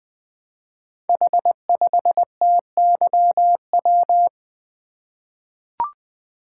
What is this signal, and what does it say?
Morse code, heard as a keyed tone, about 20 wpm, 700 Hz. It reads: H5TYW
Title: Code Groups - 5 Letter:Numbers - 20wpm